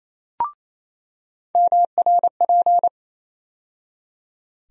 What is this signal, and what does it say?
Morse code, heard as a keyed tone, about 28 wpm, 700 Hz. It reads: MRP
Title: USA Call Sign Postfixes - 3 Letters 28wpm